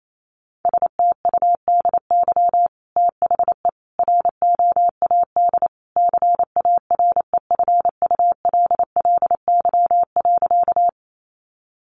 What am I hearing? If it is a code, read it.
Morse code, heard as a keyed tone, about 28 wpm, 700 Hz. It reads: STUDY THE ROAD CAREFULLY.
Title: Sentences from Top 1300 Words 28wpm